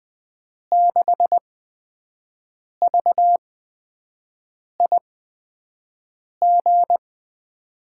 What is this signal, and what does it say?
Morse code, heard as a keyed tone, about 20 wpm, spaced as for 5 wpm, 700 Hz. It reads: 6VIG
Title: USA Call Signs - 8x Character Spacing - 20wpm